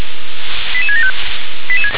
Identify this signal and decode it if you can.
Five-tone selective call (ZVEI), heard as quick runs of five tones, tones of about 70 ms. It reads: 88675
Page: ZVEI